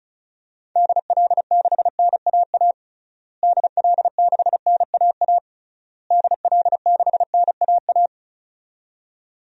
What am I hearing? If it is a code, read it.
Morse code, heard as a keyed tone, about 34 wpm, 700 Hz. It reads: DL6NAA DL6NAA DL6NAA
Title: DXCC-100 Call Signs - 6 Characters - Repeated 3x - 35wpm